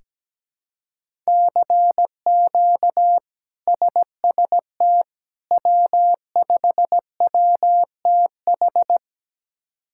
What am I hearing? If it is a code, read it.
Morse code, heard as a keyed tone, about 17 wpm, 700 Hz. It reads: CQ SST W5WTH
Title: K1USN SST QSO Practice - 17wpm